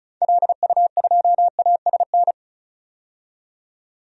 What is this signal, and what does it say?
Morse code, heard as a keyed tone, about 34 wpm, 700 Hz. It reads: LU2ASN
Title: DXCC-100 Call Signs - 6 Characters - 35wpm